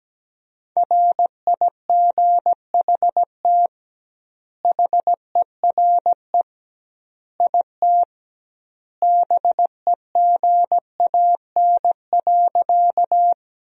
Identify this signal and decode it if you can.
Morse code, heard as a keyed tone, about 17 wpm, 700 Hz. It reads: RIGHT HERE IT BEGAN.
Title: 2x Word-Spacing - Sentences from Top 300 Words - 17wpm